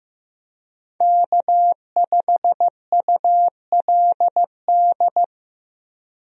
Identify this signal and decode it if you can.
Morse code, heard as a keyed tone, about 15 wpm, 700 Hz. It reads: K5ULD
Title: USA Call Signs - 5 Characters - Repeated 2x - 15wpm